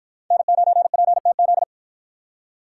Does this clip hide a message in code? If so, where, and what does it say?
Yes: Morse code from 0.3 to 1.6 s, heard as a keyed tone, about 53 wpm, 700 Hz. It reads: N9PTZ